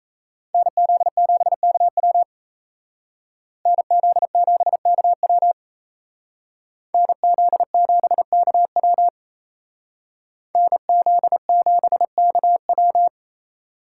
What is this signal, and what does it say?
Morse code, heard as a keyed tone, about 34 wpm, 700 Hz. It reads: NZ7KW NZ7KW NZ7KW NZ7KW
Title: Speed Racing - USA 2x2 Call Signs 28wpm